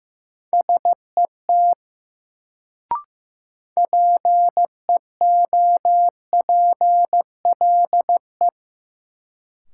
Morse code, heard as a keyed tone, about 15 wpm, 700 Hz. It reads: SET PEOPLE
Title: Sentences from Top 200 Words - New Word Drill - 15wpm